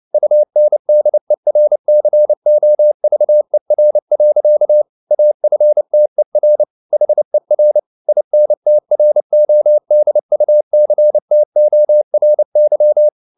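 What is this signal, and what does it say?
Morse code, heard as a keyed tone, about 29 wpm, 600 Hz. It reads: UNDERCOVER. AFTER HER INTRODUCTORY
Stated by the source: Morse code at 29 wpm, sending Ken Follett's novel Never in English, from CW med Gnister.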